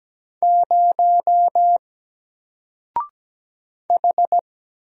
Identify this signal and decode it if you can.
Morse code, heard as a keyed tone, about 17 wpm, 700 Hz. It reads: 0 H
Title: Single Letter-Number Alt 17wpm